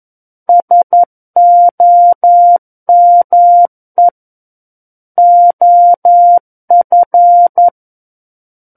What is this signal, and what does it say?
Morse code, heard as a keyed tone, about 11 wpm, 700 Hz. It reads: SOME OF
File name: 11 WPM